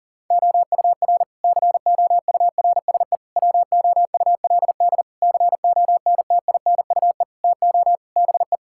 Morse code, heard as a keyed tone, about 37 wpm, 700 Hz. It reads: OUR COURSE WOULD CONTINUE TO BE
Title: Sentences from Top 1000 Words - Review Part 6 - 40wpm